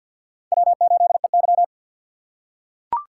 Morse code, heard as a keyed tone, about 45 wpm, 700 Hz. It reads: W8EY
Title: USA Call Signs - 4 Characters - 3x Character Spacing - 50wpm